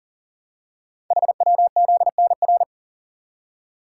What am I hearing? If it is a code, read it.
Morse code, heard as a keyed tone, about 42 wpm, 700 Hz. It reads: HWZNR
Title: Code Groups - 5 Letter:Numbers - 40wpm